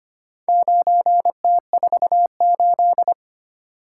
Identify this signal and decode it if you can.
Morse code, heard as a keyed tone, about 25 wpm, 700 Hz. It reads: 9T48
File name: practice / 25wpm / Four Letter-Number Combinations - 25wpm